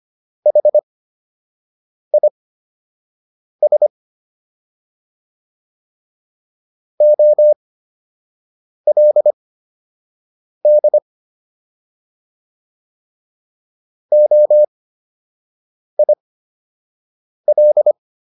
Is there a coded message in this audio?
Yes: Morse code from 0.5 to 17.9 s, heard as a keyed tone, about 25 wpm, spaced as for 6 wpm, 600 Hz. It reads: HIS OLD OIL